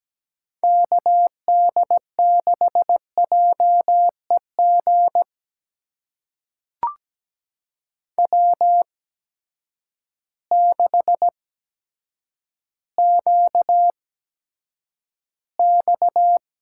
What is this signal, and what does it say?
Morse code, heard as a keyed tone, about 17 wpm, spaced as for 9 wpm, 700 Hz. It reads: KD6JEG W6QX
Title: USA Call Signs - 8x Character Spacing - 17wpm